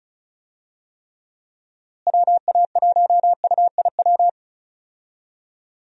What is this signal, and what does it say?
Morse code, heard as a keyed tone, about 36 wpm, 700 Hz. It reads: WA1UIW
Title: USA Call Signs - 6 Characters - Repeated 3x - 35wpm